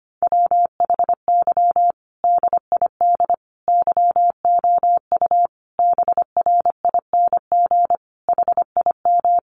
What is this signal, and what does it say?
Morse code, heard as a keyed tone, about 25 wpm, 700 Hz. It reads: WHY DID YOU BRING HIM
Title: Sentences from Top 500 Words 25wpm